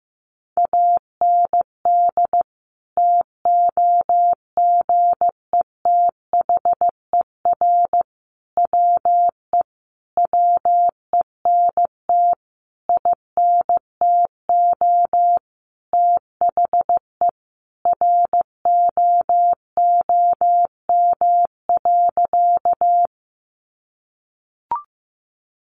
Morse code, heard as a keyed tone, about 15 wpm, 700 Hz. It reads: AND TOGETHER WE WENT INTO THE ROOM.
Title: Sentences from Top 600 Words 15wpm